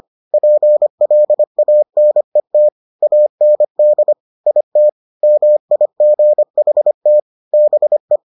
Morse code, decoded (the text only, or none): PLANET AND IT MIGHT BE